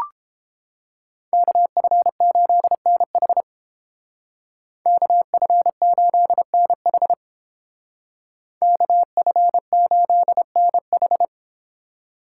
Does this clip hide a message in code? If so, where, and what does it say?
Yes: Morse code from 1.3 to 11.2 s, heard as a keyed tone, about 29 wpm, 700 Hz. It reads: KF8NH KF8NH KF8NH